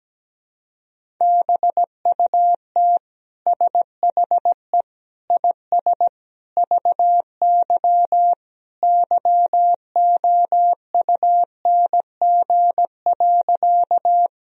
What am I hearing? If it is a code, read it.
Morse code, heard as a keyed tone, about 17 wpm, 700 Hz. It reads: BUT SHE IS VY YOUNG.
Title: Sentences from Top 400 Words and Abbreviations 17wpm